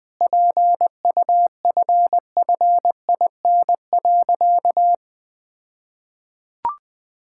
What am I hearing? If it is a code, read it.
Morse code, heard as a keyed tone, about 21 wpm, 700 Hz. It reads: PUFFIN.
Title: All-Age Appropriate Jokes - 20wpm